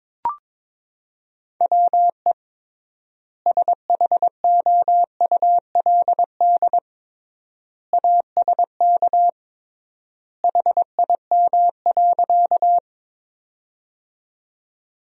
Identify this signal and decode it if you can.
Morse code, heard as a keyed tone, about 22 wpm, 700 Hz. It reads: WE SHOULD ASK HIM.
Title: 3x Word-Spacing - Sentences from Top 300 Words - 22wpm